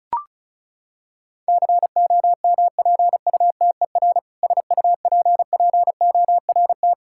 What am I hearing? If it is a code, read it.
Morse code, heard as a keyed tone, about 34 wpm, 700 Hz. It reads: COMPUTER SUPPORT
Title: K1USN SST QSO Practice 35wpm